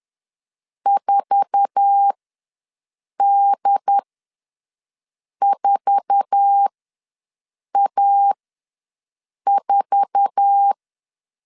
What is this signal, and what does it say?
Morse code, heard as a keyed tone, about 10 wpm, spaced as for 5 wpm, 800 Hz. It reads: 4D4A4
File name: LEZIONI / 8 YJ45 / 8 YJ4 ws 10wpm 4020